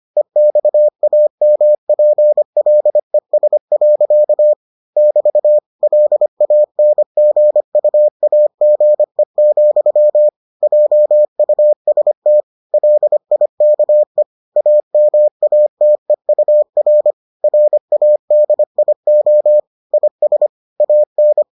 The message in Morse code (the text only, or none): EXAMPLES. <BT> LANGUAGE, JUST LIKE AMATEUR RADIO IS AN